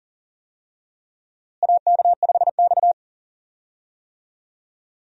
Morse code, heard as a keyed tone, about 40 wpm, 700 Hz. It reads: AK5X
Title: USA Call Signs - 4 Characters - Repeated 2x - 40wpm